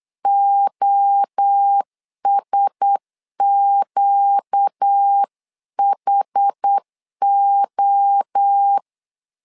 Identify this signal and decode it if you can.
Morse code, heard as a keyed tone, about 8 wpm, 800 Hz. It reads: OSQHO
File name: LEZIONI / 4 Q1S 8wpm 2710